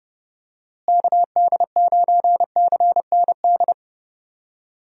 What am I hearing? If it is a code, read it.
Morse code, heard as a keyed tone, about 31 wpm, 700 Hz. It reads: KD9CND
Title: USA Call Signs - 6 Characters - Repeated 2x - 30wpm